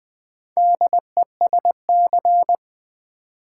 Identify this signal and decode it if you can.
Morse code, heard as a keyed tone, about 20 wpm, 700 Hz. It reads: DESC
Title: Build A Word - 2 Letters At A Time - 8-Letter Words - 20wpm